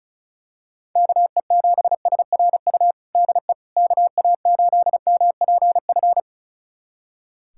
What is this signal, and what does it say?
Morse code, heard as a keyed tone, about 36 wpm, 700 Hz. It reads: KE7SRU DE KA8MPF
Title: USA Callsign DE Callsign 35wpm